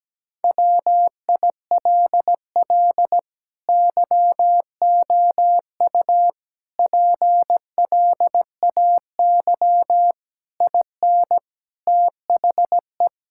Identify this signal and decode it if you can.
Morse code, heard as a keyed tone, about 17 wpm, 700 Hz. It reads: WILL YOU PLAY IN THE
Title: Sentences from Top 1600 Words 17wpm